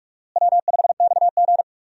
Morse code, heard as a keyed tone, about 44 wpm, 700 Hz. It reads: W5XG
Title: USA Call Signs - 4 Characters - Repeated 2x - 45wpm